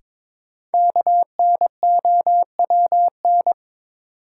Morse code, heard as a keyed tone, about 22 wpm, 700 Hz. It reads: KNOWN
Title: Sentences from Top 600 Words - New Word Drill - 22wpm